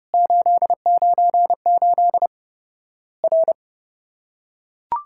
Morse code, heard as a keyed tone, about 30 wpm, 700 Hz. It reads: 898 R
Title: Worked All Europe DX Contest - QTC Exchange - 30wpm